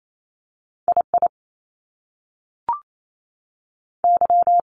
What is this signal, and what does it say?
Morse code, heard as a keyed tone, about 28 wpm, 700 Hz. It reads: I I Y
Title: Top Two Letter Combinations in Words 28wpm